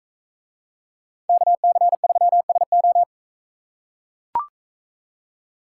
Morse code, heard as a keyed tone, about 42 wpm, 700 Hz. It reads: KC3SO